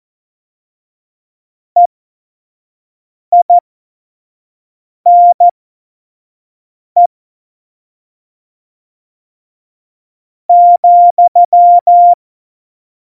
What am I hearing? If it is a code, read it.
Morse code, heard as a keyed tone, about 14 wpm, spaced as for 5 wpm, 700 Hz. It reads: EINE ,
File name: Morsekurs / Witze / Witz 28. August 2018, 25 F70